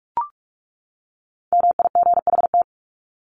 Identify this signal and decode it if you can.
Morse code, heard as a keyed tone, about 44 wpm, 700 Hz. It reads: MIGHT
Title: Top 500 Words 45wpm